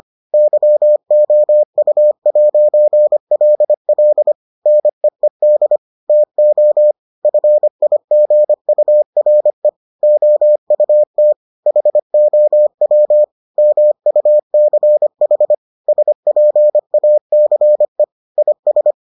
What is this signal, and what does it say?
Morse code, heard as a keyed tone, about 25 wpm, 600 Hz. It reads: YOU'LL NEED TO FIGURE OUT HOW MUCH SPACE IS